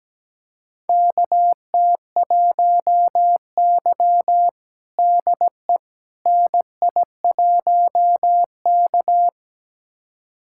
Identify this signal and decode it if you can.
Morse code, heard as a keyed tone, about 17 wpm, 700 Hz. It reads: KT1Y DE NI1K
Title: USA Callsign DE Callsign - 4 Characters - 17wpm